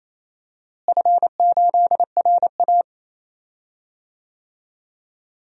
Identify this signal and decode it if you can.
Morse code, heard as a keyed tone, about 28 wpm, 700 Hz. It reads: F8RA
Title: DXCC-100 Call Signs - 4 Characters - Repeated 3x - 28wpm